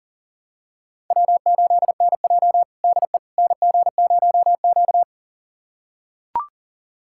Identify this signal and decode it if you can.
Morse code, heard as a keyed tone, about 40 wpm, 700 Hz. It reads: W8NJ DE NG0Q
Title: USA Callsign DE Callsign - 4 Characters - 40wpm